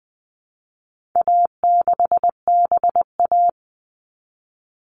Morse code, heard as a keyed tone, about 20 wpm, 700 Hz. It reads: A6BA
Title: All Cut Numbers - 4-Digit 20wpm